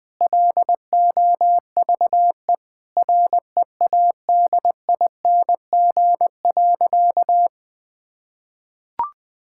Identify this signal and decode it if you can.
Morse code, heard as a keyed tone, about 20 wpm, 700 Hz. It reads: LOVE READING.
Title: Sentences with 7-Word Phrases - Spoken Beginning - 20wpm